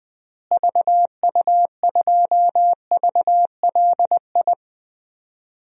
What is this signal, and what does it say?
Morse code, heard as a keyed tone, about 20 wpm, 700 Hz. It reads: VU2VLI
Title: DXCC-100 Call Signs - 6 Characters - Repeated 2x - 20wpm